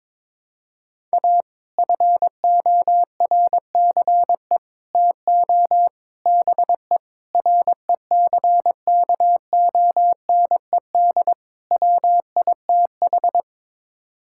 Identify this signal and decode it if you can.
Morse code, heard as a keyed tone, about 22 wpm, 700 Hz. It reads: A FORCE TO BE RECKONED WITH
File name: practice / 22wpm / Sets of 6 Words - 22wpm